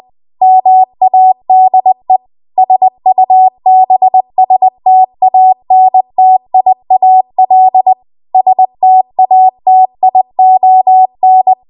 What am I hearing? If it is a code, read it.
Morse code, heard as a keyed tone, about 20 wpm, 750 Hz. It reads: MADE SUBSTANTIAL STATION
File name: Morse / 20WPM